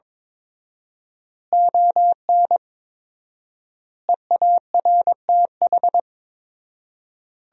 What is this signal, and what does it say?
Morse code, heard as a keyed tone, about 22 wpm, 700 Hz. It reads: ON EARTH
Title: 4x Word-Spacing - Sentences from Top 200 Words - 22wpm